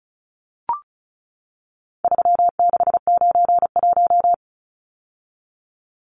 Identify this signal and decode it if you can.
Morse code, heard as a keyed tone, about 34 wpm, 700 Hz. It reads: 3691
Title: Numbers - 4-Digit 35wpm